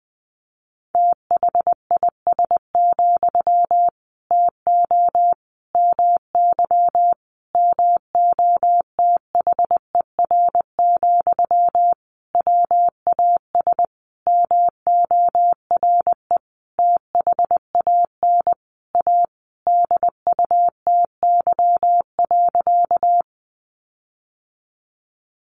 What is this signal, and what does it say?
Morse code, heard as a keyed tone, about 20 wpm, 700 Hz. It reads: THIS, TO MY MOTHER, WAS MORE THAN A DUTY.
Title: Sentences from Top 1500 Words 20wpm